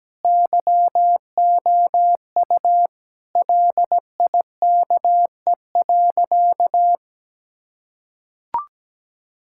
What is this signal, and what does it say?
Morse code, heard as a keyed tone, about 17 wpm, 700 Hz. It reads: YOU LIKE.
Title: Sentences from Top 1600 Words 17wpm